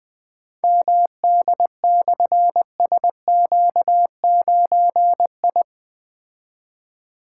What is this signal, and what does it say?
Morse code, heard as a keyed tone, about 20 wpm, 700 Hz. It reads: MD/SQ9I
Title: Difficult Call Signs 20wpm